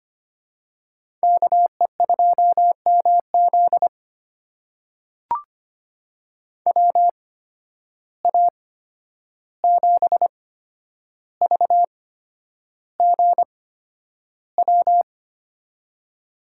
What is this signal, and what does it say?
Morse code, heard as a keyed tone, about 25 wpm, spaced as for 10 wpm, 700 Hz. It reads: KE2MZ WA7VGW
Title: USA Call Signs - 8x Character Spacing - 25wpm